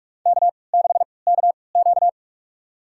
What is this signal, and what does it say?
Morse code, heard as a keyed tone, about 43 wpm, spaced as for 25 wpm, 700 Hz. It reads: K6KQ
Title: USA Call Signs - 4 Characters - 3x Character Spacing - 45wpm